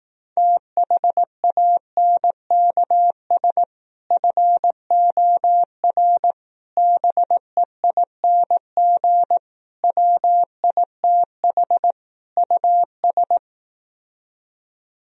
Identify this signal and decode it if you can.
Morse code, heard as a keyed tone, about 18 wpm, 700 Hz. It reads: THANKS FOR BEING WITH US
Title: Speed-Racing - Sets of 5 Words 15wpm